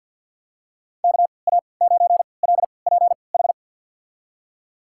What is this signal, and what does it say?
Morse code, heard as a keyed tone, about 51 wpm, spaced as for 28 wpm, 700 Hz. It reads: KA9LPH
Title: USA Call Signs - 6 Characters - 3x Character Spacing - 50wpm